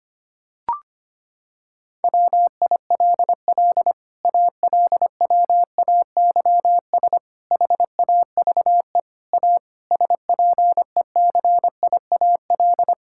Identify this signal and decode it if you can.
Morse code, heard as a keyed tone, about 25 wpm, 700 Hz. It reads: WILL ALWAYS HAVE A SPECIAL
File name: Sets of 9 Words - 25wpm